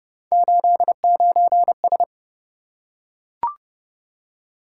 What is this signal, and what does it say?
Morse code, heard as a keyed tone, about 29 wpm, 700 Hz. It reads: 89S